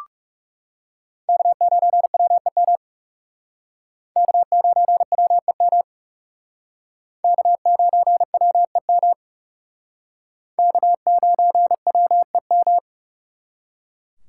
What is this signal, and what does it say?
Morse code, heard as a keyed tone, about 37 wpm, 700 Hz. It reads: K9WEM K9WEM K9WEM K9WEM
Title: Speed Racing - USA 1x3 Call Signs 30wpm